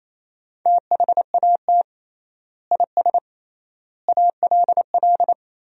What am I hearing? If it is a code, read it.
Morse code, heard as a keyed tone, about 28 wpm, 700 Hz. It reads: THAT IS ALL
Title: 3x Word-Spacing - Sentences from Top 200 Words - 28wpm